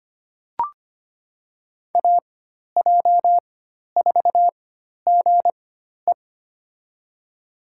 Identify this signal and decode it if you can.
Morse code, heard as a keyed tone, about 25 wpm, spaced as for 12 wpm, 700 Hz. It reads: AJ4GE
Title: USA Call Signs - 4x Character Spacing - 25wpm